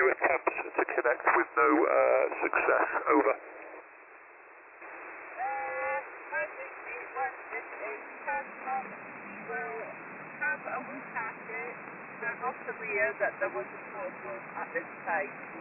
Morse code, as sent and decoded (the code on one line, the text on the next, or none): none